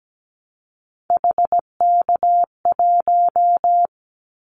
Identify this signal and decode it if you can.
Morse code, heard as a keyed tone, about 17 wpm, 700 Hz. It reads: HK1
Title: Callsign Prefixes for Top 100 DXCC - 17wpm